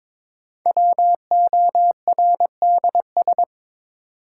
Morse code, heard as a keyed tone, about 22 wpm, 700 Hz. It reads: WORDS
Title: Sentences from Top 1000 Words - Review Part 6 - New Word Drill - 22wpm